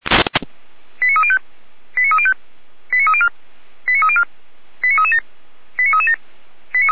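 Five-tone selective call (ZVEI), heard as quick runs of five tones, tones of about 70 ms. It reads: 89305 89305 89304 89304 89337 89337
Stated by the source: ZVEI five-tone call 89229 received by SDR.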